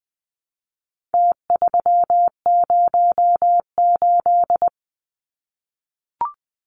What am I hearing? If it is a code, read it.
Morse code, heard as a keyed tone, about 20 wpm, 700 Hz. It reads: T308
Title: Numbers - Contest Style - 4-Digit 20wpm